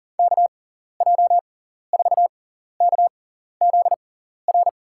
Morse code, heard as a keyed tone, about 40 wpm, spaced as for 14 wpm, 700 Hz. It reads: KJ4KZR
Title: USA Call Signs - 6 Characters - 6x Character Spacing - 40wpm